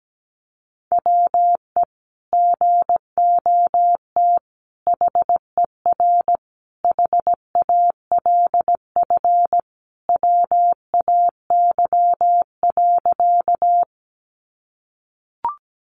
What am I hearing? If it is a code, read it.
Morse code, heard as a keyed tone, about 17 wpm, 700 Hz. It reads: WE GOT HER HALF WAY.
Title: Sentences from Top 400 Words 17wpm